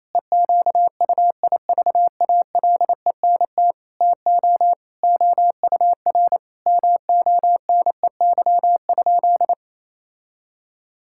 Morse code, heard as a keyed tone, about 28 wpm, 700 Hz. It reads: EQUIVALENT TO OUR MONEY?
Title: Sentences from Top 1600 Words 28wpm